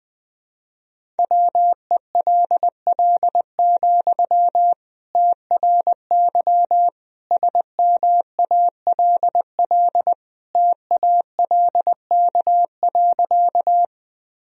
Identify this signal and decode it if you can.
Morse code, heard as a keyed tone, about 20 wpm, 700 Hz. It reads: WELL, TRY SMALL TALK.
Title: Sentences from Top 1000 Words - Review Part 1 - 20wpm